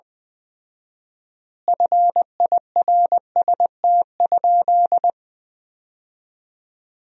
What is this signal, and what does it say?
Morse code, heard as a keyed tone, about 20 wpm, 700 Hz. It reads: FIRST?
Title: 4x Word-Spacing - Sentences from Top 100 Words - 20wpm